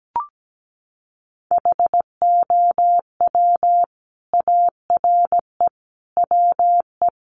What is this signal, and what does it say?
Morse code, heard as a keyed tone, about 17 wpm, 700 Hz. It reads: HOW ARE WE
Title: Sets of 5 Words Encore 17wpm